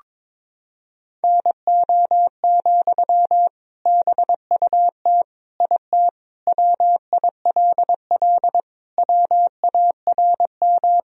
Morse code, heard as a keyed tone, about 21 wpm, 700 Hz. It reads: NO, BUT IT WILL WARM
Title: Sentences from Top 1000 Words - Review Part 2 - 22wpm